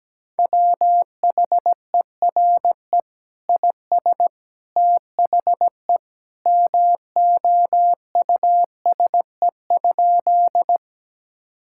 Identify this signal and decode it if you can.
Morse code, heard as a keyed tone, about 17 wpm, 700 Hz. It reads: WHERE IS THE MOUSE?
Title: Sentences from Top 1000 Words - Review Part 5 - 17wpm